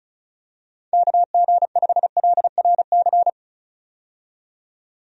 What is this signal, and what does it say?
Morse code, heard as a keyed tone, about 36 wpm, 700 Hz. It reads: KG5LRC
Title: USA Call Signs - 6 Characters - 35wpm